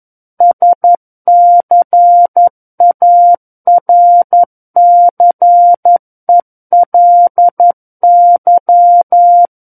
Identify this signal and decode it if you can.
Morse code, heard as a keyed tone, about 11 wpm, 700 Hz. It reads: SCARCELY
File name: 11 WPM